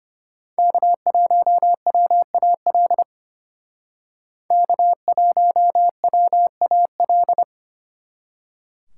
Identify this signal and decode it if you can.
Morse code, heard as a keyed tone, about 32 wpm, 700 Hz. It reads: K1WAL K1WAL